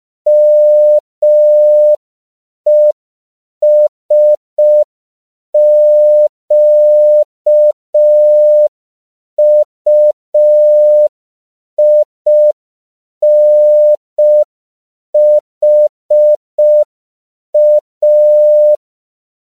Morse code, heard as a keyed tone, about 5 wpm, 600 Hz. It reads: MESQUINHA